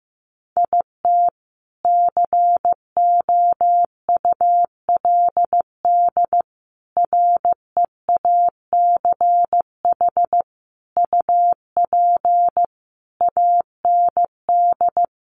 Morse code, heard as a keyed tone, about 15 wpm, 700 Hz. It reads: IT COULD REACH UP AND
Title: A Sound Of Thunder 15wpm